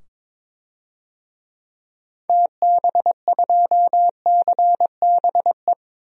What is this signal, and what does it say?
Morse code, heard as a keyed tone, about 22 wpm, 700 Hz. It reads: TB2CBE
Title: DXCC-100 Call Signs - 6 Characters - 22wpm